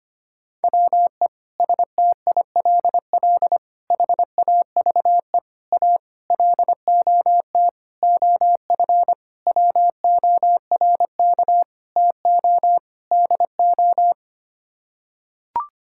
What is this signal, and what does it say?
Morse code, heard as a keyed tone, about 25 wpm, 700 Hz. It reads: WE STILL HAVE A LOT OF WORK TO DO
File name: Sets of 9 Words - 25wpm